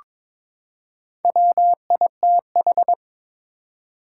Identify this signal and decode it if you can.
Morse code, heard as a keyed tone, about 22 wpm, 700 Hz. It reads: WITH
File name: practice / 22wpm / Build A Phrase - 5-Word Phrases - 22wpm